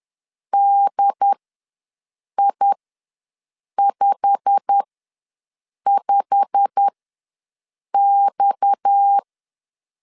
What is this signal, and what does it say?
Morse code, heard as a keyed tone, about 10 wpm, spaced as for 6 wpm, 800 Hz. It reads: DI55X
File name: LEZIONI / 8 YJ45 / 8 YJ45 ws 10wpm 3843